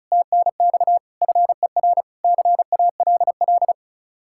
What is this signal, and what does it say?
Morse code, heard as a keyed tone, about 45 wpm, 700 Hz. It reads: TNX FER CALL